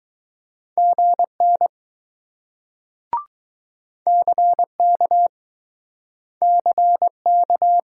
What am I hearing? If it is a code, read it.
Morse code, heard as a keyed tone, about 20 wpm, 700 Hz. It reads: GN CK CK